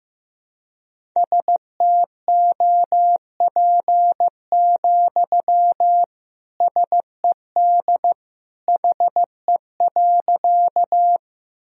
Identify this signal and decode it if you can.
Morse code, heard as a keyed tone, about 15 wpm, 700 Hz. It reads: STOP, SED HE.
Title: Sentences from Top 300 Words and Abbreviations 15wpm